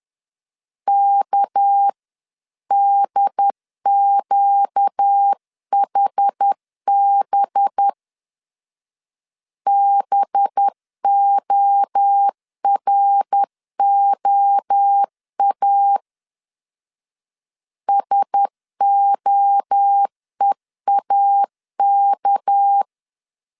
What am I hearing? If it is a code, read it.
Morse code, heard as a keyed tone, about 10 wpm, 800 Hz. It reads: KDQHB BOROA SOEAK
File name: CW / DET / 1 6 1500C S B 10wpm 4137